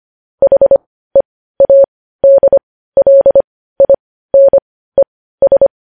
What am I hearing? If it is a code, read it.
Morse code, heard as a keyed tone, about 25 wpm, spaced as for 15 wpm, 550 Hz. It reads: HEADLINES
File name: podcast / News headlines-1x-25x15